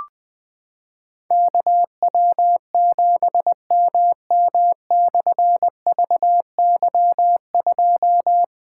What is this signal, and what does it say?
Morse code, heard as a keyed tone, about 21 wpm, 700 Hz. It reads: KW7MM/VY2
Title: Difficult Call Signs 20wpm